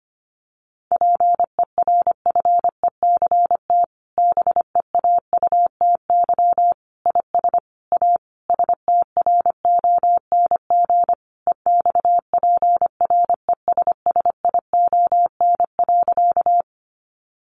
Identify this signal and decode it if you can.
Morse code, heard as a keyed tone, about 25 wpm, 700 Hz. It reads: PERFECT BEAUTY IS A STRONG EXPRESSION.
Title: Sentences from Top 1200 Words 25wpm